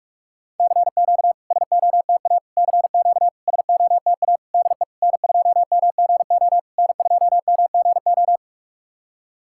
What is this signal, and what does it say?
Morse code, heard as a keyed tone, about 44 wpm, 690 Hz. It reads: CQ SOTA CQ SOTA DE N2MGO N2MGO
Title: SOTA QSO Practice 45wpm